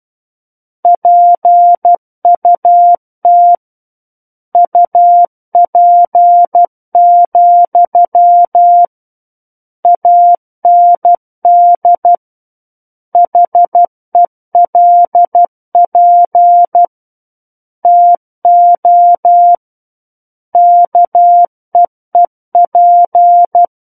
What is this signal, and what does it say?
Morse code, heard as a keyed tone, about 12 wpm, 700 Hz. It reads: PUT UP, AND HELP TO KEEP
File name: Book One / 03-Chapter 3 - 12 WPM